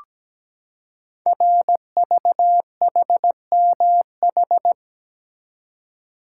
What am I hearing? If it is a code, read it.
Morse code, heard as a keyed tone, about 17 wpm, 700 Hz. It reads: RVHMH
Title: Code Groups - 5 Letters - 17wpm